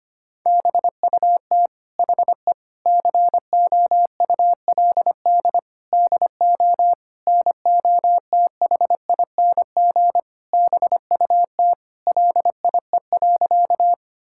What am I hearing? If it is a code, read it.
Morse code, heard as a keyed tone, about 25 wpm, 700 Hz. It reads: BUT HE COULD DO NOTHING BUT LIE.
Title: Sentences from Top 1600 Words 25wpm